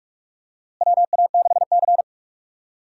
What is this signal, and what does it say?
Morse code, heard as a keyed tone, about 41 wpm, 700 Hz. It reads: WA6C
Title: USA Call Signs - 4 Characters - Repeated 3x - 45wpm